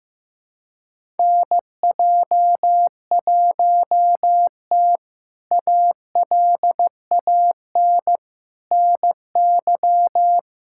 Morse code, heard as a keyed tone, about 15 wpm, 700 Hz. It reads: NJ1T ALAN NY
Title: K1USN SST QSO Practice - 15wpm